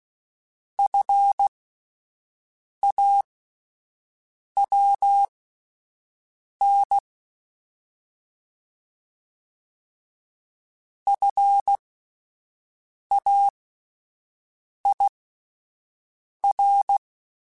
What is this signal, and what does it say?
Morse code, heard as a keyed tone, about 16 wpm, spaced as for 5 wpm, 800 Hz. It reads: FAWN FAIR
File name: Alphabetical / Alphabetical mp3s / Fs @5